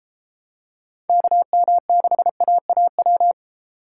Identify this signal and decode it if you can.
Morse code, heard as a keyed tone, about 29 wpm, 700 Hz. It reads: KM6AAW